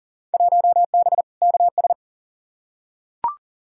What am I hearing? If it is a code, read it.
Morse code, heard as a keyed tone, about 40 wpm, 700 Hz. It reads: 1B KS
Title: Field Day Exchanges Encore - 40wpm